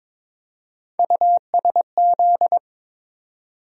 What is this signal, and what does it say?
Morse code, heard as a keyed tone, about 22 wpm, 700 Hz. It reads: USZ
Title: USA Call Sign Postfixes - 3 Letters 22wpm